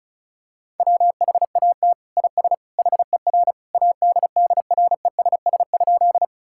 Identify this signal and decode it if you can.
Morse code, heard as a keyed tone, about 34 wpm, 700 Hz. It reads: WHAT IS HER ADDRESS?
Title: Sentences from Top 1000 Words - Review Part 6 - 35wpm